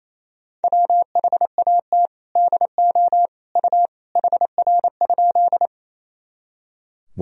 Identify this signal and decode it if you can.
Morse code, heard as a keyed tone, about 28 wpm, 700 Hz. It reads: WHAT DO U HR?
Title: Sentences from Top 300 Words and Abbreviations 28wpm